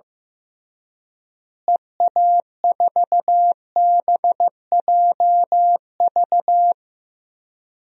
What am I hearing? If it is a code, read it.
Morse code, heard as a keyed tone, about 16 wpm, 700 Hz. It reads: EA4BJV
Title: DXCC-100 Call Signs - 6 Characters - Repeated 3x - 15wpm